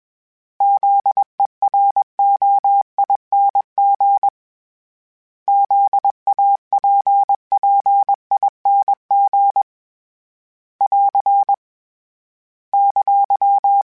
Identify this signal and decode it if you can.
Morse code, heard as a keyed tone, about 20 wpm, 800 Hz. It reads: ZEROING ZAPPING <AR> !
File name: Special INGs / Special INGs-mp3s / Sngl Ing @20